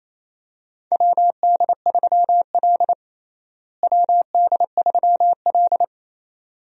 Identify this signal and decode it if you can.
Morse code, heard as a keyed tone, about 28 wpm, 700 Hz. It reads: WD3L WD3L
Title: USA Call Signs - 4 Characters - Repeated 2x - 28wpm